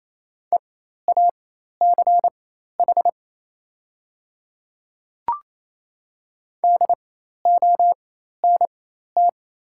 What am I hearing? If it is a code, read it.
Morse code, heard as a keyed tone, about 29 wpm, spaced as for 13 wpm, 700 Hz. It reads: EACH DONT
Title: 4x Character Spacing - Top 500 Words - 28wpm